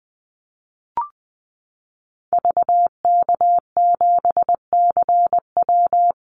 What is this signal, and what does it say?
Morse code, heard as a keyed tone, about 19 wpm, 700 Hz. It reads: VK7CW
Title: Call Signs Repeated 2x - 20wpm